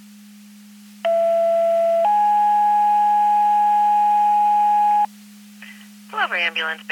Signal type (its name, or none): two-tone page